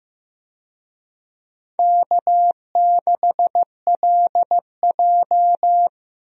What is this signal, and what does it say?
Morse code, heard as a keyed tone, about 15 wpm, 700 Hz. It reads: K6LJ
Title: USA Call Signs - 4 Characters - Repeated 3x - 15wpm